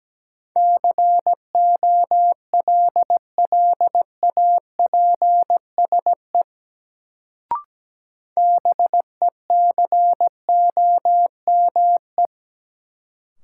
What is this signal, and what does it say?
Morse code, heard as a keyed tone, about 17 wpm, 700 Hz. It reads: COLLAPSE BECOME
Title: Sets of 5 Words Encore - New Word Drill - 17wpm